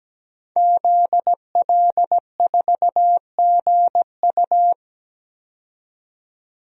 Morse code, heard as a keyed tone, about 17 wpm, 700 Hz. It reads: ZL4GU
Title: DXCC-100 Call Signs - 5 Characters - 17wpm